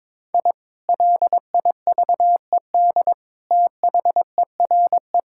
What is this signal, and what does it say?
Morse code, heard as a keyed tone, about 22 wpm, 700 Hz. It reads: I LIVED THERE
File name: practice / 22wpm / Sentences with 3-Word Phrases - 22wpm